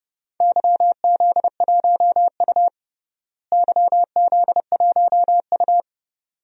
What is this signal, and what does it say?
Morse code, heard as a keyed tone, about 30 wpm, 700 Hz. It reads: YZ1U YZ1U
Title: DXCC-100 Call Signs - 4 Characters - Repeated 2x - 30wpm